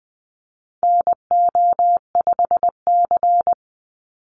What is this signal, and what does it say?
Morse code, heard as a keyed tone, about 20 wpm, 700 Hz. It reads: NO5C
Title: USA Call Signs 20wpm